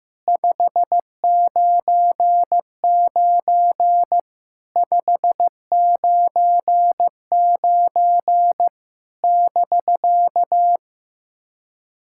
Morse code, heard as a keyed tone, about 15 wpm, 700 Hz. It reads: 599 599 <BK>
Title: SOTA QSO Practice 15wpm